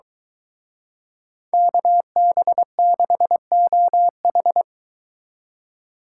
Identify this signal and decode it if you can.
Morse code, heard as a keyed tone, about 23 wpm, 700 Hz. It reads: KB6OH